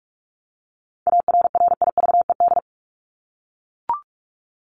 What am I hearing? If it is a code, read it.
Morse code, heard as a keyed tone, about 44 wpm, 700 Hz. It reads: ARRIVED
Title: ED Suffix - Top 100 Words with Base Words 45wpm